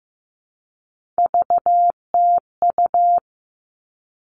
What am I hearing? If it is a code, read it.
Morse code, heard as a keyed tone, about 15 wpm, 700 Hz. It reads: VTU
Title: All Cut Numbers - 3-Digit 15wpm